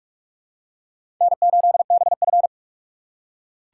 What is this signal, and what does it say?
Morse code, heard as a keyed tone, about 41 wpm, 700 Hz. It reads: N8BF
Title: USA Call Signs - 4 Characters - 3x Character Spacing - 45wpm